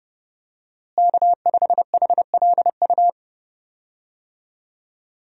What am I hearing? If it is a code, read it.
Morse code, heard as a keyed tone, about 30 wpm, 700 Hz. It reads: K5HLU
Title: USA Call Signs - 5 Characters - Repeated 2x - 30wpm